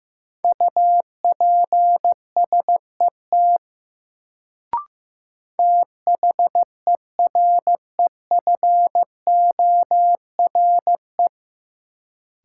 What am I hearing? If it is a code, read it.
Morse code, heard as a keyed tone, about 15 wpm, 700 Hz. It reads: UPSET THEREFORE
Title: Sentences from Top 800 Words - New Word Drill - 15wpm